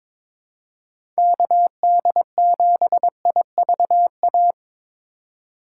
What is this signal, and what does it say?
Morse code, heard as a keyed tone, about 23 wpm, 700 Hz. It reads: KD7IVA
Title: USA Call Signs - 6 Characters - 3x Character Spacing - 22wpm